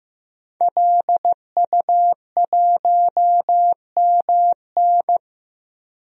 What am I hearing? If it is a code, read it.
Morse code, heard as a keyed tone, about 15 wpm, 700 Hz. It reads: LU1MN
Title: DXCC-100 Call Signs - 5 Characters - Repeated 2x - 15wpm